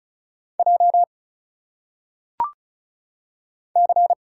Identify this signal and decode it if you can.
Morse code, heard as a keyed tone, about 34 wpm, 700 Hz. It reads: J C
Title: Single Letters - Phonetic Pronunciation 35wpm